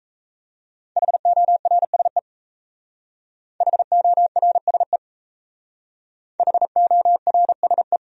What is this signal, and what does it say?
Morse code, heard as a keyed tone, about 33 wpm, 700 Hz. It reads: HORSE HORSE HORSE
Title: Speed-Racing Top 500 Words 28wpm